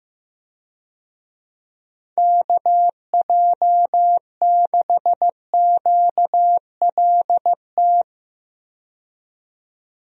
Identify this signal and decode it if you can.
Morse code, heard as a keyed tone, about 15 wpm, 700 Hz. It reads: KJ6QLT
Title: USA Call Signs - 6 Characters - 15wpm